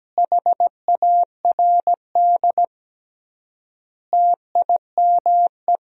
Morse code, heard as a keyed tone, about 17 wpm, 700 Hz. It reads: HARD TIME
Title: 3x Word-Spacing - Sentences from Top 300 Words - 17wpm